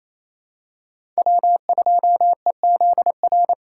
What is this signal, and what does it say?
Morse code, heard as a keyed tone, about 28 wpm, 700 Hz. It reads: W2EZR
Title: USA Call Signs - 5 Characters - 2x Character Spacing - 28wpm